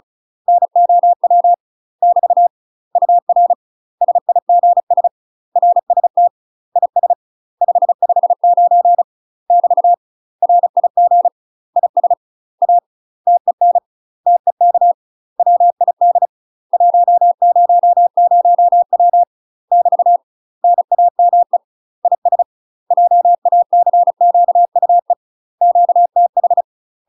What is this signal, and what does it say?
Morse code, heard as a keyed tone, about 34 wpm, 700 Hz. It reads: NOW <BT> UR SIGS RST IS 559 <BT> RIG IS A TEN TEK WID 100W <BT> NAME IS JACQUE QTH
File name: UK MASTER QSOs TRAINING / 35WPM / UK MASTER QSOs 0034